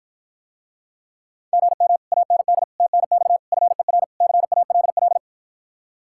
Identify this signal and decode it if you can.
Morse code, heard as a keyed tone, about 54 wpm, 700 Hz. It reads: GM AND TNX FER CALL